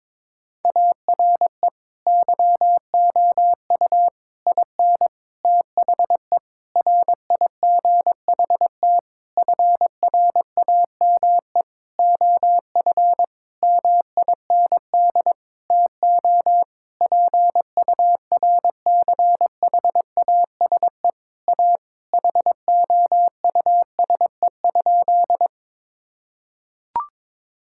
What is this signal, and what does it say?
Morse code, heard as a keyed tone, about 22 wpm, 700 Hz. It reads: ARE YOU IN THE RIGHT FRAME OF MIND TO PURCHASE A HOUSE?
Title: Sentences from Top 1000 Words - Review Part 6 - 22wpm